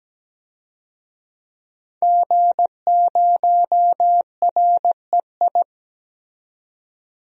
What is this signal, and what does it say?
Morse code, heard as a keyed tone, about 17 wpm, 700 Hz. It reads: G0REI
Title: DXCC-100 Call Signs - 5 Characters - 17wpm